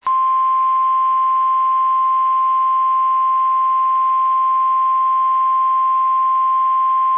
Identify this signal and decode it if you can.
Emergency Alert System (SAME) tones (the 1050 Hz attention tone). It is heard from 0.5 to 6.9 s.
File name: eas / KSDZKDJL / KSDZKDJL